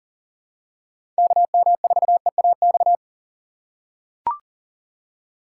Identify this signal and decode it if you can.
Morse code, heard as a keyed tone, about 40 wpm, 700 Hz. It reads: KM4EAX
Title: USA Call Signs - 6 Characters - 4x Character Spacing - 40wpm